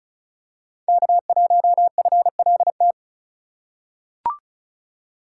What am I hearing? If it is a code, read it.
Morse code, heard as a keyed tone, about 36 wpm, 700 Hz. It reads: K1FLT
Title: USA Call Signs - 5 Characters - 6x Character Spacing - 35wpm